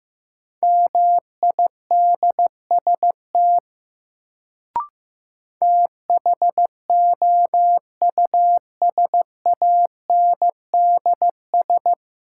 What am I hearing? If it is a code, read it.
Morse code, heard as a keyed tone, about 15 wpm, 700 Hz. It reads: MIDST THOUSANDS
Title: Sets of 4 Words - New Word Drill - 15wpm